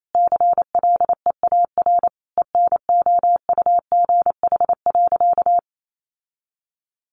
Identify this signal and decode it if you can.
Morse code, heard as a keyed tone, about 28 wpm, 700 Hz. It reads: CLEAR ENOUGH.
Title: Sentences from Top 1500 Words 28wpm